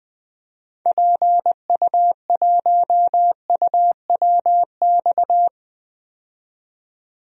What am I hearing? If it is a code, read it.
Morse code, heard as a keyed tone, about 20 wpm, 700 Hz. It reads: PU1UWX
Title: DXCC-100 Call Signs - 6 Characters - 20wpm